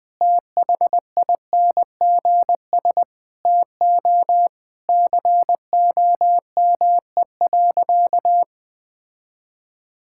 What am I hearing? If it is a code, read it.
Morse code, heard as a keyed tone, about 20 wpm, 700 Hz. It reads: THINGS TO COME.
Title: Sentences with 6-Word Phrases - 20wpm